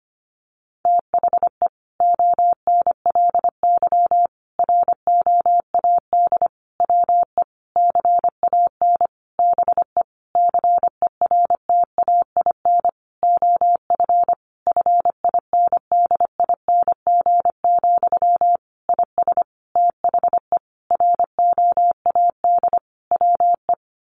Morse code, heard as a keyed tone, about 25 wpm, 700 Hz. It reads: THE ONLY ROAD WE CAN BE CERTAIN OF FINDING, IS THE ROAD WE
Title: Sentences from Top 1100 Words 25wpm